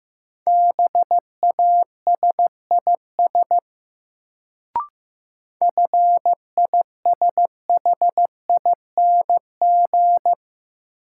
Morse code, heard as a keyed tone, about 15 wpm, 700 Hz. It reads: BASIS FISHING
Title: Sentences from Top 1000 Words - Review Part 5 - New Word Drill - 15wpm